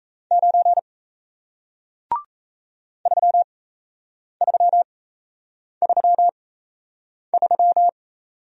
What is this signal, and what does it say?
Morse code, heard as a keyed tone, about 37 wpm, spaced as for 8 wpm, 700 Hz. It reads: 9 3333